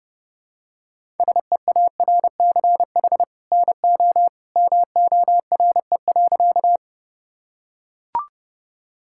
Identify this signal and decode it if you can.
Morse code, heard as a keyed tone, about 32 wpm, 700 Hz. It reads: SEARCH NO MORE.
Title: Sentences from Top 1000 Words - Review Part 5 - 30wpm